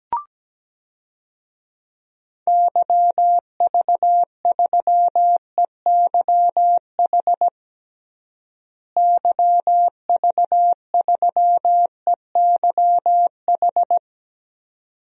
Morse code, heard as a keyed tone, about 17 wpm, 700 Hz. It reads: YV3EYH YV3EYH
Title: DXCC-100 Call Signs - 6 Characters - Repeated 2x - 17wpm